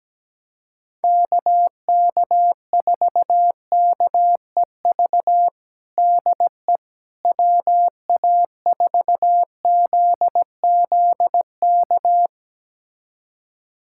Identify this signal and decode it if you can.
Morse code, heard as a keyed tone, about 17 wpm, 700 Hz. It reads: KK4KEV DE WA4ZZK
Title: USA Callsign DE Callsign - 6 Characters - 17wpm